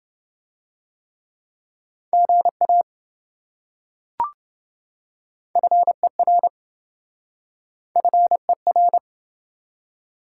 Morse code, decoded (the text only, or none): GA FER FER